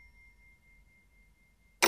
There is no code.